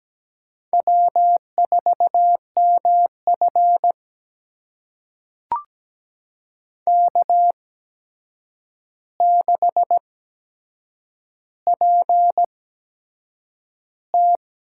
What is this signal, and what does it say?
Morse code, heard as a keyed tone, about 17 wpm, spaced as for 7 wpm, 700 Hz. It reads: W4MF K6PT
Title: USA Call Signs - 4 Characters - 8x Character Spacing - 17wpm